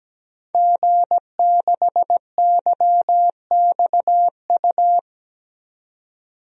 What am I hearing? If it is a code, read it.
Morse code, heard as a keyed tone, about 17 wpm, 700 Hz. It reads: G6YXU
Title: DXCC-100 Call Signs - 5 Characters - 17wpm